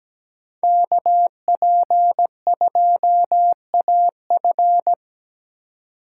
Morse code, heard as a keyed tone, about 17 wpm, 700 Hz. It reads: KP2AF
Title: USA Call Signs - 5 Characters - 3x Character Spacing - 17wpm